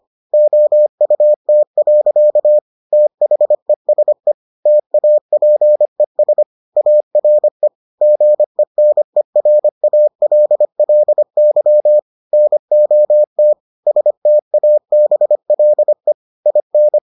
Morse code, heard as a keyed tone, about 25 wpm, 600 Hz. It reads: OUT. THESE TAPES ARE GENERALLY NOT STABLE IN